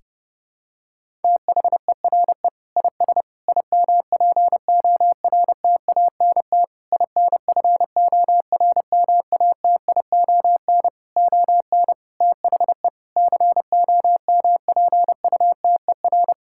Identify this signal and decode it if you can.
Morse code, heard as a keyed tone, about 30 wpm, 700 Hz. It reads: THERE IS IMPORTANT INFORMATION ON THE COMPUTER
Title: Sentences from Top 1000 Words - Review Part 6 - 30wpm